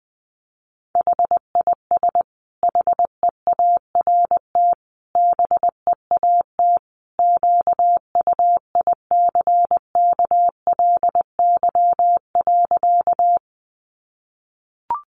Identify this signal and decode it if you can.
Morse code, heard as a keyed tone, about 20 wpm, 700 Hz. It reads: HIS HEART BEAT QUICKLY.
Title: Sentences from Top 1300 Words 20wpm